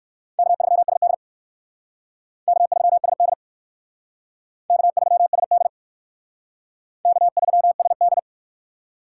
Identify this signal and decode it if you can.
Morse code, heard as a keyed tone, about 57 wpm, 700 Hz. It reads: K3SD K3SD K3SD K3SD